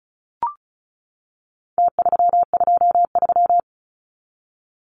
Morse code, heard as a keyed tone, about 34 wpm, 700 Hz. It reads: T323
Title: Numbers - Contest Style - 4-Digit 35wpm